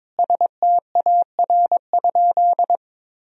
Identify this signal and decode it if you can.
Morse code, heard as a keyed tone, about 22 wpm, 700 Hz. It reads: STAR?
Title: Sentences from Top 1000 Words - Review Part 2 - 22wpm